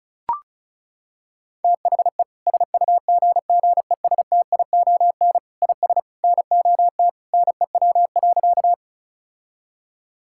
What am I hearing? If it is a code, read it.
Morse code, heard as a keyed tone, about 34 wpm, 700 Hz. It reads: THE SUGGESTION IS NOT NEW.
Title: Sentences from Top 1600 Words 35wpm